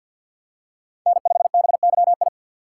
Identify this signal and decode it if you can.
Morse code, heard as a keyed tone, about 47 wpm, 700 Hz. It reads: N5BYI
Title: USA Call Signs - 5 Characters - 6x Character Spacing - 50wpm